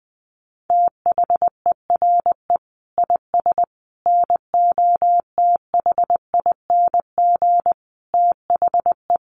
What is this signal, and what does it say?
Morse code, heard as a keyed tone, about 20 wpm, 700 Hz. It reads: THERE IS NOTHING THE
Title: Sentences from Top 700 Words 20wpm